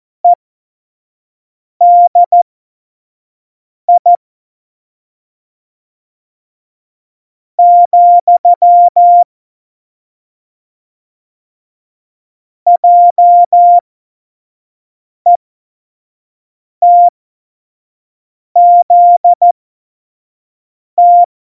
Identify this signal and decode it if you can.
Morse code, heard as a keyed tone, about 14 wpm, spaced as for 4 wpm, 700 Hz. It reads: EDI, JETZT
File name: Morsekurs / Witze / Witz 27. Februar 2018, 25 F70